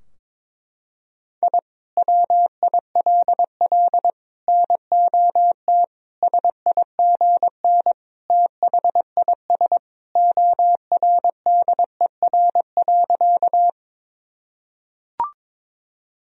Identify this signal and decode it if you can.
Morse code, heard as a keyed tone, about 21 wpm, 700 Hz. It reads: I WILL NOT SIGN THIS ORDER.
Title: Sentences from Top 1000 Words - Review Part 2 - 22wpm